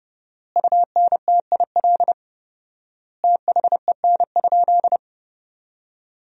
Morse code, heard as a keyed tone, about 29 wpm, 700 Hz. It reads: UNTIL THEN?
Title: 4x Word-Spacing - Sentences from Top 500 Words - 30wpm